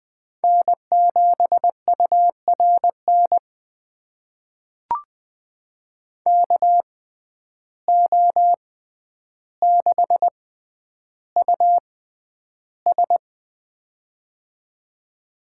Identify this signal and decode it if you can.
Morse code, heard as a keyed tone, about 20 wpm, spaced as for 10 wpm, 700 Hz. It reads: N7URN KO6US
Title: USA Call Signs - 6x Character Spacing - 20wpm